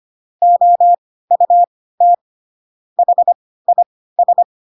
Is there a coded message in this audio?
Yes: Morse code from 0.4 to 4.4 s, heard as a keyed tone, about 25 wpm, spaced as for 16 wpm, 700 Hz. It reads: OUT HIS